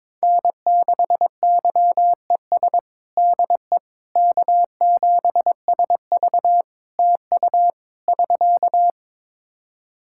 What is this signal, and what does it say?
Morse code, heard as a keyed tone, about 22 wpm, 700 Hz. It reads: N6YES DE K7SV TU <SK>
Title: K1USN SST QSO Practice 22wpm